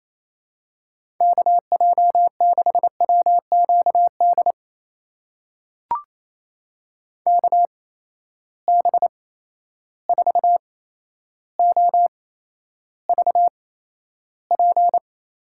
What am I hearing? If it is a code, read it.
Morse code, heard as a keyed tone, about 28 wpm, spaced as for 12 wpm, 700 Hz. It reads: KJ6WQD KB4OVP
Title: USA Call Signs - 6 Characters - 8x Character Spacing - 28wpm